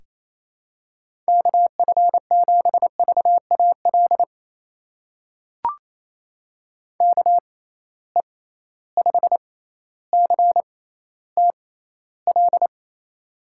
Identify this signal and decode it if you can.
Morse code, heard as a keyed tone, about 28 wpm, spaced as for 14 wpm, 700 Hz. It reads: KF7VAL KE5CTL
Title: USA Call Signs - 6 Characters - 6x Character Spacing - 28wpm